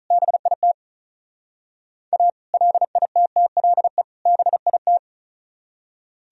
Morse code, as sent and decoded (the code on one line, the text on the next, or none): -... .. - / .- / .-.. .. - - .-.. / -... .. -
BIT A LITTLE BIT